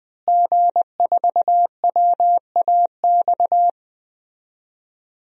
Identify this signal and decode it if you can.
Morse code, heard as a keyed tone, about 20 wpm, 700 Hz. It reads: G4WAX
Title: DXCC-100 Call Signs - 5 Characters - Repeated 3x - 20wpm